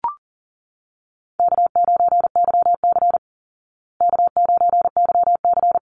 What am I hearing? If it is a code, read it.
Morse code, heard as a keyed tone, about 37 wpm, 700 Hz. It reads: K9YC K9YC
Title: Easy Call Signs Repeated 3x - 40wpm